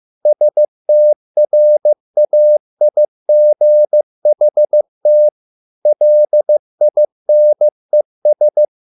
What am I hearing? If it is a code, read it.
Morse code, heard as a keyed tone, about 15 wpm, 600 Hz. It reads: STRAIGHT LINES